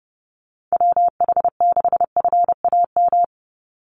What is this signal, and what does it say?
Morse code, heard as a keyed tone, about 30 wpm, 700 Hz. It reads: WH6FAM
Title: USA Call Signs 30wpm